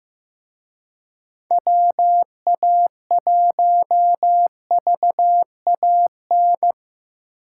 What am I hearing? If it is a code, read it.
Morse code, heard as a keyed tone, about 18 wpm, 700 Hz. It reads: WA1VAN
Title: Speed Racing - USA 2x3 Call Signs 15wpm